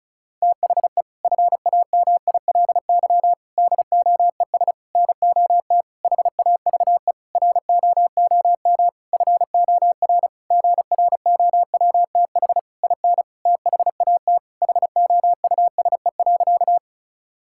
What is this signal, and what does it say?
Morse code, heard as a keyed tone, about 36 wpm, 700 Hz. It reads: THE FAMILY DOES NOT HAVE ROOM FOR GROWTH IN THAT HOUSE.
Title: Sentences from Top 1000 Words - Review Part 5 - 35wpm